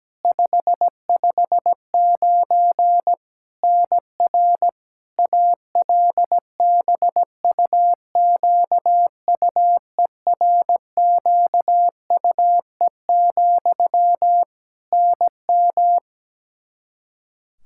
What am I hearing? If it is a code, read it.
Morse code, heard as a keyed tone, about 17 wpm, 700 Hz. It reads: 559 NR ALBUQUERQUE, NM
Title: Practice QSO - Exchange 1 - Beginning - 17wpm